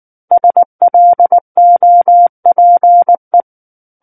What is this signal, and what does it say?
Morse code, heard as a keyed tone, about 19 wpm, 700 Hz. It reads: SLOPE